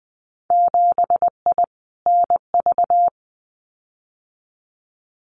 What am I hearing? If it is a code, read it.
Morse code, heard as a keyed tone, about 20 wpm, 700 Hz. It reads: 7I NV
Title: Winter Field Day Exchanges Encore 20wpm